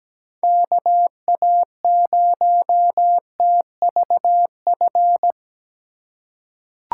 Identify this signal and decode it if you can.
Morse code, heard as a keyed tone, about 17 wpm, 700 Hz. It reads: KA0TVF
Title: USA Call Signs - 6 Characters - 3x Character Spacing - 17wpm